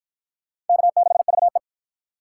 Morse code, heard as a keyed tone, about 50 wpm, 700 Hz. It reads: K6VE